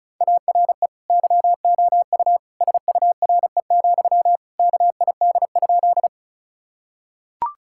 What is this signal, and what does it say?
Morse code, heard as a keyed tone, about 36 wpm, 700 Hz. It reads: ARE YOU SURE, KID?
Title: Sentences from Top 1600 Words 35wpm